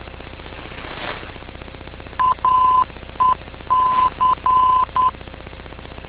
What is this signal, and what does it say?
Morse code, heard as a keyed tone, about 10 wpm, 1020 Hz. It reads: AEC
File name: info / aec vor